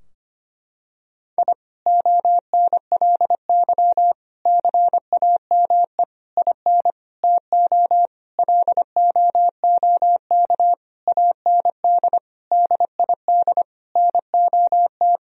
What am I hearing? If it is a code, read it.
Morse code, heard as a keyed tone, about 25 wpm, 700 Hz. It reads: I ONLY CAME IN TO LOOK AND DID NOT
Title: Sentences from Top 1000 Words - Review Part 2 - 25wpm